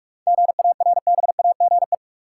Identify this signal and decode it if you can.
Morse code, heard as a keyed tone, about 44 wpm, 700 Hz. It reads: GARBAGE
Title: Seven Letter Words - 45wpm